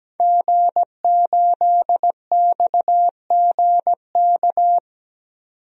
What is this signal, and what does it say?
Morse code, heard as a keyed tone, about 17 wpm, 700 Hz. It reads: G8XGK
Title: DXCC-100 Call Signs - 5 Characters - 17wpm